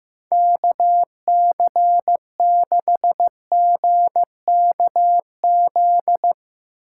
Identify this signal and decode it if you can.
Morse code, heard as a keyed tone, about 15 wpm, 700 Hz. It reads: KC6GKZ
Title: USA Call Signs - 6 Characters - 8x Character Spacing - 15wpm